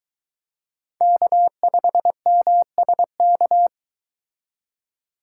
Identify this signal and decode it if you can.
Morse code, heard as a keyed tone, about 23 wpm, 700 Hz. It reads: K5MSK